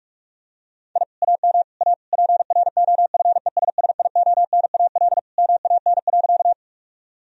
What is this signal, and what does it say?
Morse code, heard as a keyed tone, about 46 wpm, 700 Hz. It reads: I AM A PROFESSIONAL MAN.
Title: Sentences from Top 1000 Words - Review Part 4 - 45wpm